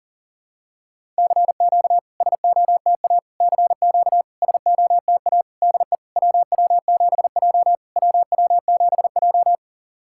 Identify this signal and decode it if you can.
Morse code, heard as a keyed tone, about 40 wpm, 690 Hz. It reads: CQ SOTA CQ SOTA DE WW7J WW7J
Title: SOTA QSO Practice 40wpm